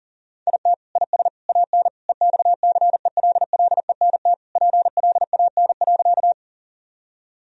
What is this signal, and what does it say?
Morse code, heard as a keyed tone, about 40 wpm, 700 Hz. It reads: IT IS AN EXCELLENT PLAN.
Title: Sentences from Top 1000 Words - Review Part 2 - 40wpm